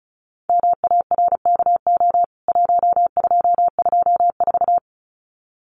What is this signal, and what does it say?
Morse code, heard as a keyed tone, about 34 wpm, 700 Hz. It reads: MARKO 1224
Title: CWT Exchange 35wpm